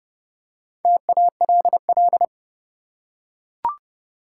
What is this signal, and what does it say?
Morse code, heard as a keyed tone, about 32 wpm, 700 Hz. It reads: TALL
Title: Four Letter Words - 30wpm